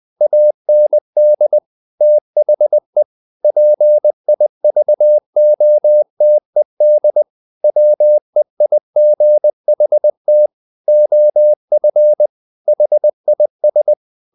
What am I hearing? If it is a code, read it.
Morse code, heard as a keyed tone, about 20 wpm, 600 Hz. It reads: AND THE PIVOTED WEIGHT OF HIS
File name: the old man and the sea / the old man and the sea part 1 20wpm